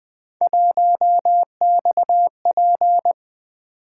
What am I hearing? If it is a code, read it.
Morse code, heard as a keyed tone, about 20 wpm, 700 Hz. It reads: 1XP
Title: Three Letter-Number Combinations - 20wpm